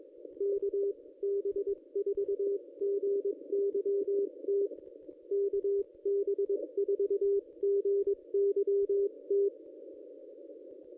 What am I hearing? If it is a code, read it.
Morse code, heard as a keyed tone, about 20 wpm, 410 Hz. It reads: KB4GYT KB4GYT
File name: contests / ss-cw-k5tr-op-n5rz -1200